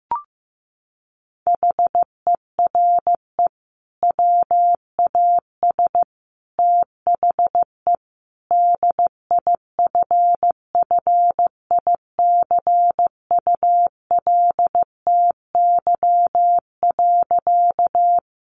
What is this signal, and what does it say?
Morse code, heard as a keyed tone, about 15 wpm, 700 Hz. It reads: HERE WAS THE DIFFICULTY.
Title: Sentences from Top 1500 Words 15wpm